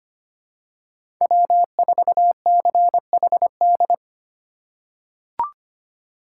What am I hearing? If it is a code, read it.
Morse code, heard as a keyed tone, about 25 wpm, 700 Hz. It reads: W4CHD
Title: USA Call Signs - 5 Characters - 2x Character Spacing - 25wpm